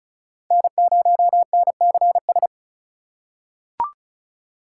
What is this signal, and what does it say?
Morse code, heard as a keyed tone, about 36 wpm, 700 Hz. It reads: N0NCS
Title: USA Call Signs - 5 Characters - 4x Character Spacing - 35wpm